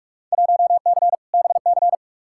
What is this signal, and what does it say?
Morse code, heard as a keyed tone, about 46 wpm, 700 Hz. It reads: 1C BC
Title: Field Day Exchanges Encore - 45wpm